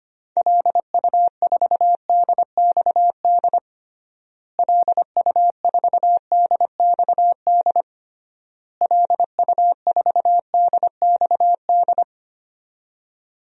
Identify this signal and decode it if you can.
Morse code, heard as a keyed tone, about 25 wpm, 700 Hz. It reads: LU4DXD LU4DXD LU4DXD
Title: DXCC-100 Call Signs - 6 Characters - Repeated 3x - 25wpm